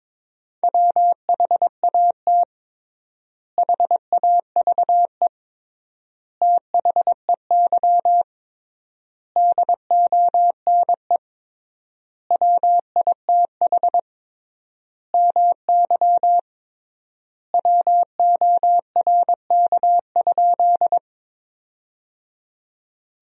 Morse code, heard as a keyed tone, about 22 wpm, 700 Hz. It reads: WHAT HAVE THEY DONE WITH MY WORK?
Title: 3x Word-Spacing - Sentences from Top 500 Words - 22wpm